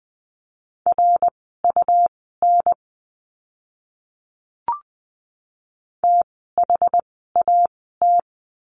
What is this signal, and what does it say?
Morse code, heard as a keyed tone, about 20 wpm, spaced as for 15 wpm, 700 Hz. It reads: RUN THAT
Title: 237 Lesson 38f - Letter q - Words 20wpm